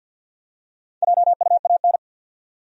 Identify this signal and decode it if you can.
Morse code, heard as a keyed tone, about 49 wpm, 700 Hz. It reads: JUAN
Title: USA Common Names - 50wpm